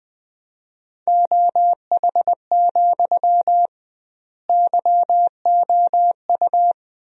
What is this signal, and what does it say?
Morse code, heard as a keyed tone, about 20 wpm, 700 Hz. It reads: OH, YOU
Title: 2x Word-Spacing - Sentences from Top 500 Words - 20wpm